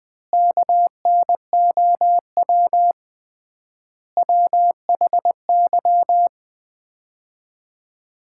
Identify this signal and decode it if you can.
Morse code, heard as a keyed tone, about 20 wpm, 700 Hz. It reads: KNOW WHY
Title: 3x Word-Spacing - Sets of 4 Words - 20wpm